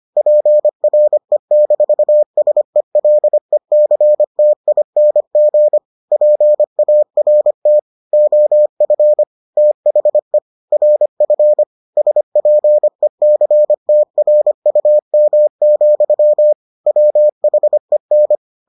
Morse code, heard as a keyed tone, about 25 wpm, 600 Hz. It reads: PRE-SELECTING PART OF THE RF SPECTRUM, WHEN